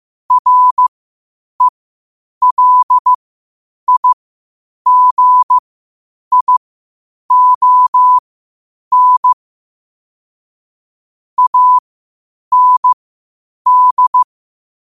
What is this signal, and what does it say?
Morse code, heard as a keyed tone, about 15 wpm, spaced as for 8 wpm, 1000 Hz. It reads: RELIGION AND
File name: qotdfiles / 8wpm